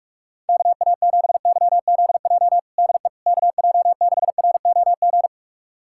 Morse code, heard as a keyed tone, about 42 wpm, 700 Hz. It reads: KA7YZJ DE KJ6ROG
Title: USA Callsign DE Callsign - 6 Characters - 45wpm